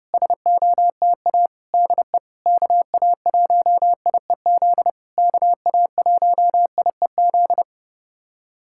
Morse code, heard as a keyed tone, about 30 wpm, 700 Hz. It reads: SOTA DE KA1IEZ KA1IEZ
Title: SOTA QSO Practice 30wpm